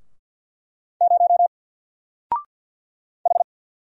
Morse code, heard as a keyed tone, about 50 wpm, 700 Hz. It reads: 0 H
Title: Single Letter-Number Alt 50wpm